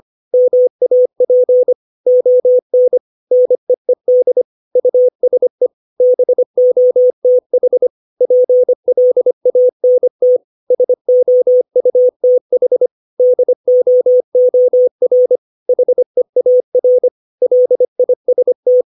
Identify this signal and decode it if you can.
Morse code, heard as a keyed tone, about 25 wpm, 500 Hz. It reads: MAP ON NEED USE BOTH PLANT SOUTH DOOR HEAR LIST
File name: 632 words / 632 Words 25WPM